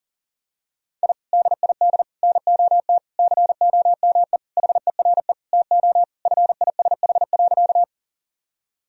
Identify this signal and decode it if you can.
Morse code, heard as a keyed tone, about 40 wpm, 700 Hz. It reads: I DID NOT COME HERE TO FISH.
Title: Sentences from Top 1000 Words - Review Part 1 - 40wpm